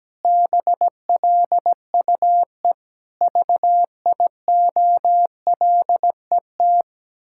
Morse code, heard as a keyed tone, about 17 wpm, 700 Hz. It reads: BLUE VIOLET
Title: Pangram Challenge v2 - 17wpm